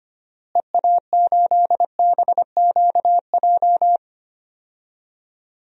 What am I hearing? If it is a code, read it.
Morse code, heard as a keyed tone, about 25 wpm, 700 Hz. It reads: EA8BQJ
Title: DXCC-100 Call Signs - 6 Characters - Repeated 2x - 25wpm